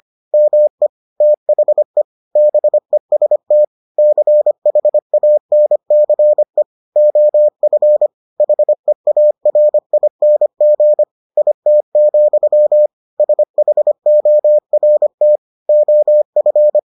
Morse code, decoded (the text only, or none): ME THE BEST CHANCE OF HEARING IT, SHORT OF